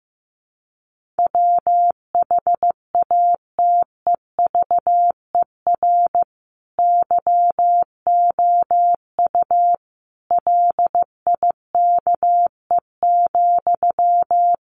Morse code, heard as a keyed tone, about 15 wpm, 700 Hz. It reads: WHATEVER YOU LIKE,
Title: Sentences from Top 1100 Words 15wpm